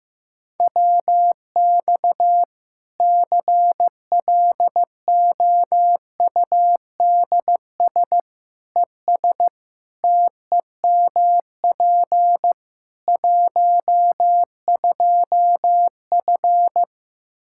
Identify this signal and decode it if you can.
Morse code, heard as a keyed tone, about 15 wpm, 700 Hz. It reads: WX CLOUDS ES TEMP 12F
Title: K1USN SST QSO Practice 15wpm